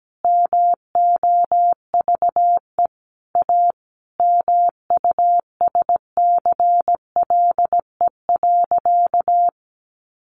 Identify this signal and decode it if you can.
Morse code, heard as a keyed tone, about 17 wpm, 700 Hz. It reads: MOVE A MUSCLE.
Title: Sentences from Top 1300 Words 17wpm